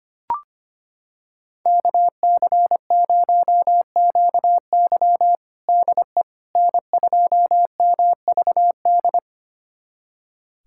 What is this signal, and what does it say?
Morse code, heard as a keyed tone, about 25 wpm, 700 Hz. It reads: KC0QY DE N2MVD
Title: USA Callsign DE Callsign - 5 Characters - 25wpm